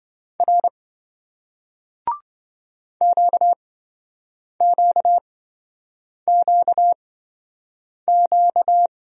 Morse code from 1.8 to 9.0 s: QQQQ

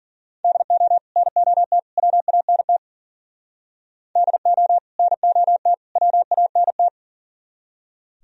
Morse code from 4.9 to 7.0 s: NOT WANT

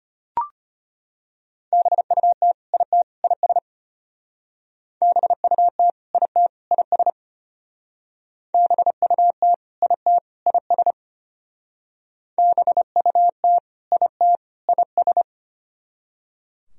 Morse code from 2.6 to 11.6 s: IT IS BUT IT IS BUT IT IS

